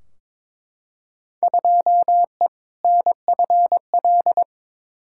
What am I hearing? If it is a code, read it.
Morse code, heard as a keyed tone, about 22 wpm, 700 Hz. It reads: 2E NFL
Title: Field Day Exchanges Encore - 22wpm